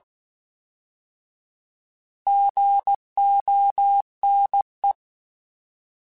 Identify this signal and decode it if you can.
Morse code, heard as a keyed tone, about 16 wpm, 800 Hz. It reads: GONE